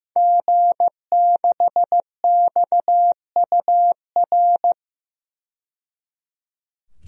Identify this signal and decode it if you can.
Morse code, heard as a keyed tone, about 15 wpm, 700 Hz. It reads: G6XUR
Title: DXCC-100 Call Signs - 5 Characters - 15wpm